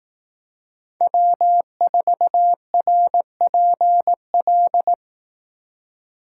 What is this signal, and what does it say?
Morse code, heard as a keyed tone, about 18 wpm, 700 Hz. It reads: W4RPL
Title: Speed Racing - USA 1x3 Call Signs 15wpm